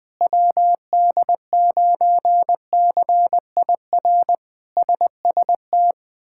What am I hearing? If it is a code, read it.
Morse code, heard as a keyed tone, about 20 wpm, 700 Hz. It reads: WD9CIR SST
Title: K1USN SST QSO Practice - 20wpm